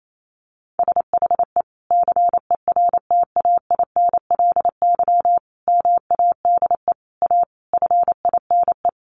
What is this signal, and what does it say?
Morse code, heard as a keyed tone, about 28 wpm, 700 Hz. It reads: SHE CERTAINLY MADE A FINE
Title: Sentences from Top 1100 Words 28wpm